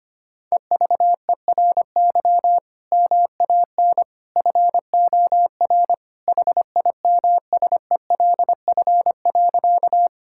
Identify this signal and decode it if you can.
Morse code, heard as a keyed tone, about 25 wpm, 700 Hz. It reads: EVERY MAN FOR HIMSELF.
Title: Sentences from Top 1600 Words 25wpm